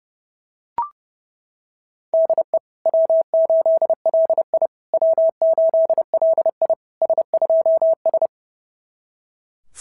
Morse code, heard as a keyed tone, about 27 wpm, 660 Hz. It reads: DE W8LI W8LI S2S
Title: SOTA QSO Practice 30wpm